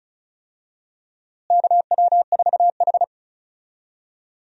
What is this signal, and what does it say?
Morse code, heard as a keyed tone, about 36 wpm, 700 Hz. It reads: KW4H
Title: USA Call Signs - 4 Characters - Repeated 3x - 35wpm